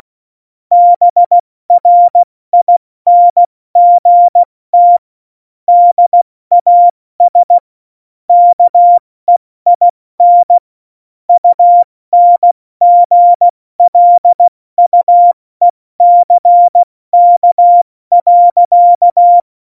Morse code, heard as a keyed tone, about 16 wpm, 700 Hz. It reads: BRINGT DAS KEIN UNGLUECK.
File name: Morsekurs / Witze / Witz 13. August 2019, 70 F80